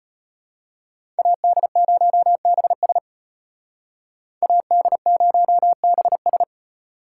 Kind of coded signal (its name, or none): Morse code